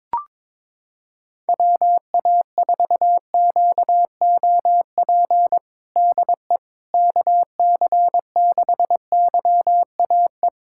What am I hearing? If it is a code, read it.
Morse code, heard as a keyed tone, about 22 wpm, 700 Hz. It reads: WA4QOP DE KC6YAE
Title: USA Callsign DE Callsign 22wpm